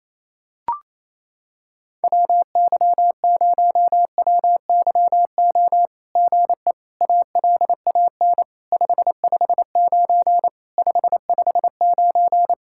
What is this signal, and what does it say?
Morse code, heard as a keyed tone, about 28 wpm, 700 Hz. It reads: WY0WYO GE ALAN 559 559
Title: SOTA QSO Practice 28wpm